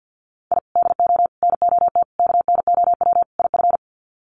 Morse code, heard as a keyed tone, about 50 wpm, 700 Hz. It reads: I DO NOT KNOW IF
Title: Sets of 5 Words 50wpm